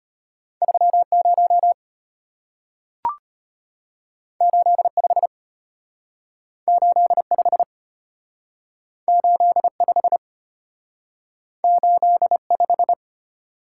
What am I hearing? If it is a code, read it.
Morse code, heard as a keyed tone, about 33 wpm, 700 Hz. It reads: 30 85 85 85 85